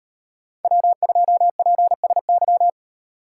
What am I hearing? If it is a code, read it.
Morse code, heard as a keyed tone, about 38 wpm, 700 Hz. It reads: W2PSY